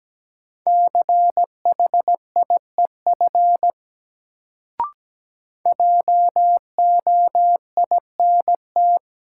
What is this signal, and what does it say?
Morse code, heard as a keyed tone, about 17 wpm, 700 Hz. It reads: CHIEF JOINT
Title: Sets of 5 Words Encore - New Word Drill - 17wpm